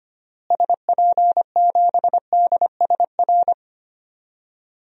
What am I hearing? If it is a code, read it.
Morse code, heard as a keyed tone, about 25 wpm, 700 Hz. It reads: SP7DSR
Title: DXCC-100 Call Signs - 6 Characters - Repeated 3x - 25wpm